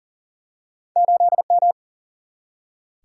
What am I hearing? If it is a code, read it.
Morse code, heard as a keyed tone, about 40 wpm, 700 Hz. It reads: 8M
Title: Single-Letter Single-Number 40wpm